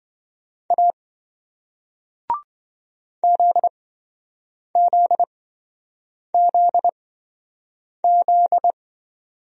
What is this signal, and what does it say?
Morse code, heard as a keyed tone, about 27 wpm, spaced as for 7 wpm, 700 Hz. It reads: A ZZZZ